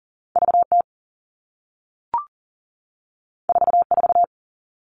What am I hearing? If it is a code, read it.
Morse code, heard as a keyed tone, about 41 wpm, 700 Hz. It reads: VT 44
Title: All Cut Numbers - 2-Digit 40wpm